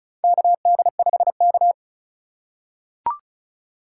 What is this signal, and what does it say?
Morse code, heard as a keyed tone, about 36 wpm, 700 Hz. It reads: KD5K
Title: USA Call Signs - 4 Characters - 3x Character Spacing - 35wpm